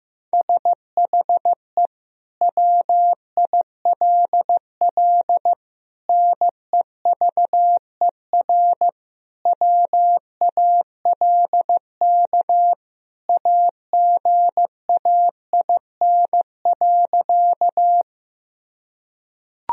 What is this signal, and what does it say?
Morse code, heard as a keyed tone, about 15 wpm, 700 Hz. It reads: SHE WILL NEVER WALK AGAIN.
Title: Sentences from Top 1000 Words - Review Part 1 - 15wpm